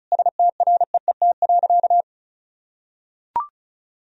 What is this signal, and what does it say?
Morse code, heard as a keyed tone, about 34 wpm, 700 Hz. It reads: STREET.
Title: Sentences from Top 1000 Words - Review Part 4 - 35wpm